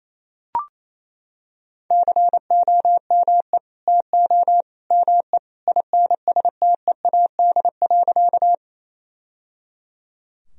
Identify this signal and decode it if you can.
Morse code, heard as a keyed tone, about 27 wpm, 700 Hz. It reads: COME TO ME INSTEAD.
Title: Sentences from Top 1000 Words - Review Part 4 - 28wpm